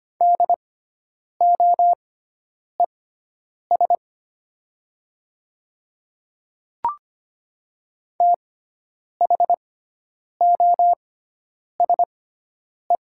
Morse code, heard as a keyed tone, about 25 wpm, spaced as for 9 wpm, 700 Hz. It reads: DOES THOSE
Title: 6x Character Spacing - Top 500 Words - 25wpm